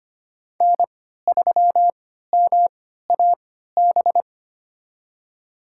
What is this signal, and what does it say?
Morse code, heard as a keyed tone, about 25 wpm, spaced as for 14 wpm, 700 Hz. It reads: N3MAB
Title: USA Call Signs - 5 Characters - 3x Character Spacing - 25wpm